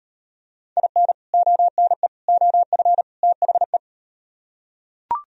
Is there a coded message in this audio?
Yes: Morse code from 0.8 to 3.8 s, heard as a keyed tone, about 35 wpm, 700 Hz. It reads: IN ONE OF THE